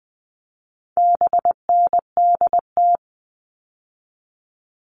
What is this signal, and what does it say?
Morse code, heard as a keyed tone, about 20 wpm, 700 Hz. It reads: BNDT
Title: All Cut Numbers - 4-Digit 20wpm